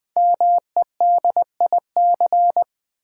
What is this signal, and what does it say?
Morse code, heard as a keyed tone, about 20 wpm, 700 Hz. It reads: MEDIC
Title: Build A Word - 5 Letters At A Time - 10-Letter Words - 20wpm